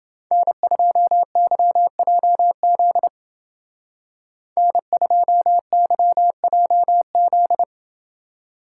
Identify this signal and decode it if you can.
Morse code, heard as a keyed tone, about 28 wpm, 700 Hz. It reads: N2YJZ N2YJZ